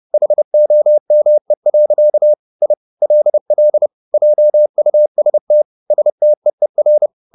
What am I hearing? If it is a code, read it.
Morse code, heard as a keyed tone, about 30 wpm, 600 Hz. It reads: HOME. I LL JUST STEER